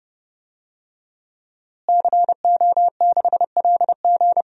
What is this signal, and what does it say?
Morse code, heard as a keyed tone, about 32 wpm, 700 Hz. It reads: CO6LG
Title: DXCC-100 Call Signs - 5 Characters - Repeated 3x - 30wpm